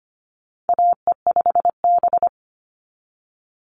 Morse code, heard as a keyed tone, about 25 wpm, 700 Hz. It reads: AE5B
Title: NA SOTA Chaser Callsigns 25wpm